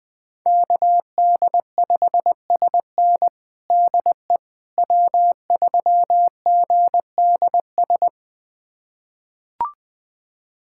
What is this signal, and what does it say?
Morse code, heard as a keyed tone, about 21 wpm, 700 Hz. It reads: KD5SN DE W3GDS
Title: USA Callsign DE Callsign 20wpm